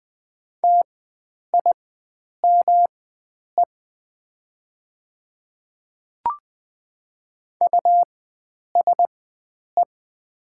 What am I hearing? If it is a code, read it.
Morse code, heard as a keyed tone, about 21 wpm, spaced as for 9 wpm, 690 Hz. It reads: TIME USE
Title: 4x Character Spacing - Top 100 Words - 20wpm